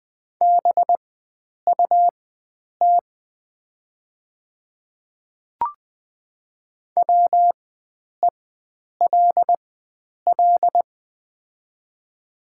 Morse code, heard as a keyed tone, about 20 wpm, spaced as for 9 wpm, 700 Hz. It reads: BUT WELL
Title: 4x Character Spacing - Top 300 Words - 20wpm